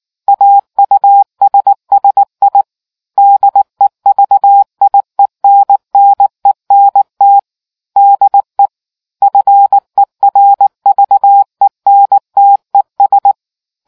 Morse code, heard as a keyed tone, about 19 wpm, 800 Hz. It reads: AUSSI DEVIENNENT DE FERVENTES